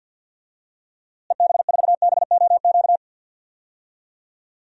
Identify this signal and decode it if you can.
Morse code, heard as a keyed tone, about 52 wpm, 700 Hz. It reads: EB4BOQ
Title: DXCC-100 Call Signs - 6 Characters - Repeated 3x - 50wpm